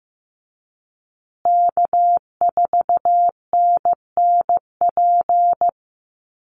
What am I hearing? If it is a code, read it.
Morse code, heard as a keyed tone, about 15 wpm, 700 Hz. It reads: K4NNP
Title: USA Call Signs Repeated 3x 15wpm